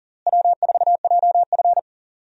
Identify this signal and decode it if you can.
Morse code, heard as a keyed tone, about 37 wpm, 700 Hz. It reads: W4JF
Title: USA Call Signs - 4 Characters - Repeated 3x - 40wpm